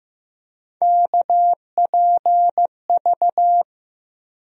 Morse code, heard as a keyed tone, about 15 wpm, 700 Hz. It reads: KPV
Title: USA Call Sign Postfixes - 3 Letters 15wpm